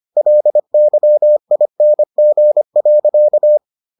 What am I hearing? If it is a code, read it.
Morse code, heard as a keyed tone, about 25 wpm, 600 Hz. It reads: LYING.